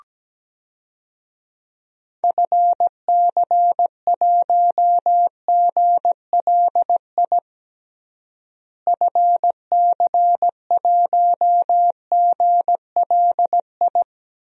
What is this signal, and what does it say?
Morse code, heard as a keyed tone, about 17 wpm, 700 Hz. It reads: FC1GLI FC1GLI
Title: DXCC-100 Call Signs - 6 Characters - Repeated 2x - 17wpm